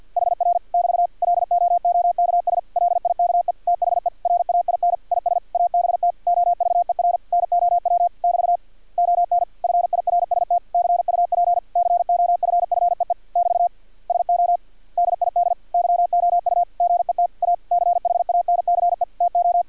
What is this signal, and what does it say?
Morse code, heard as a keyed tone, about 50 wpm, 700 Hz. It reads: HR <BT> LOOKS LIKE THE RAIN IS ABT OVER NOW <BT> ON FIRST CUP COFFEE <BT> SO DID YOU GET A CHANCE TO
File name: archived / CRSnet